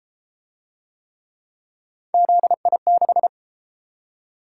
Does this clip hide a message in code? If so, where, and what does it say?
Yes: Morse code from 2.1 to 3.3 s, heard as a keyed tone, about 33 wpm, 700 Hz. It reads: ZI6